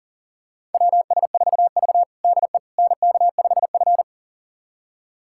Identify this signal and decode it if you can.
Morse code, heard as a keyed tone, about 40 wpm, 700 Hz. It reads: WS4V DE NK5F
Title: USA Callsign DE Callsign - 4 Characters - 40wpm